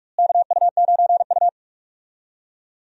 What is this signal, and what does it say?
Morse code, heard as a keyed tone, about 41 wpm, 700 Hz. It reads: KU9U
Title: USA Call Signs - 4 Characters - Repeated 3x - 45wpm